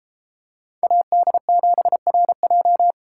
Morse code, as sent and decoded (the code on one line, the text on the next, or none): .- -.. --... .-. .---
AD7RJ